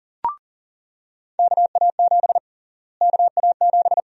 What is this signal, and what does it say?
Morse code, heard as a keyed tone, about 40 wpm, 700 Hz. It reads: KA7 KA7
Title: USA Call Sign Prefixes - Repeated 2x - 40wpm